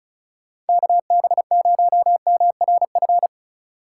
Morse code, heard as a keyed tone, about 36 wpm, 700 Hz. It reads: KB0MRF
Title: USA Call Signs - 6 Characters - 3x Character Spacing - 35wpm